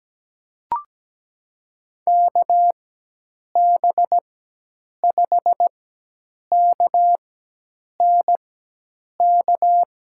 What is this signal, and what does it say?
Morse code, heard as a keyed tone, about 17 wpm, spaced as for 8 wpm, 700 Hz. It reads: KB5KNK
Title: USA Call Signs - 6 Characters - 4x Character Spacing - 17wpm